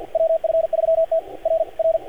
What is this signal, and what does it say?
Morse code, heard as a keyed tone, about 45 wpm, 590 Hz, with radio noise. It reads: WF3T R R
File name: archived / CRSnet